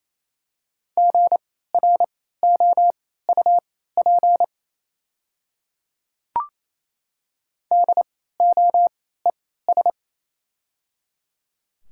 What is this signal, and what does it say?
Morse code, heard as a keyed tone, about 28 wpm, spaced as for 16 wpm, 700 Hz. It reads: GROUP DOES
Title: 3x Character Spacing - Top 300 Words - 28wpm